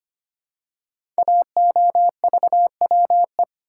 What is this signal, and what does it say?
Morse code, heard as a keyed tone, about 25 wpm, 700 Hz. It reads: AOVWE
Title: Code Groups - 5 Letters - 25wpm